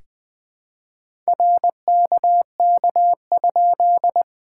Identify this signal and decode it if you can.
Morse code, heard as a keyed tone, about 20 wpm, 660 Hz. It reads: RKK?
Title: POTA QSO Practice - 20wpm